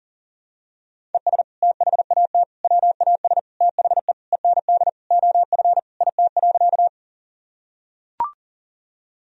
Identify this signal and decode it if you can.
Morse code, heard as a keyed tone, about 37 wpm, 700 Hz. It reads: ES THAT WAS THE END OF IT.
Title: Sentences from Top 200 Words and Abbreviations 40wpm